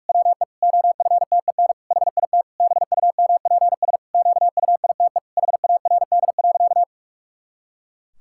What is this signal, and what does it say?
Morse code, heard as a keyed tone, about 43 wpm, 700 Hz. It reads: WE OFTEN HIT BUMPS QUITE HARD.
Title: Pangram Challenge v2 - 45wpm